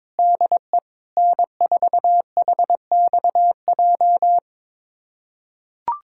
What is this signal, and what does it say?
Morse code, heard as a keyed tone, about 22 wpm, 700 Hz. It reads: DE N4HXJ
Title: USA Callsign DE Callsign - 5 Characters - 22wpm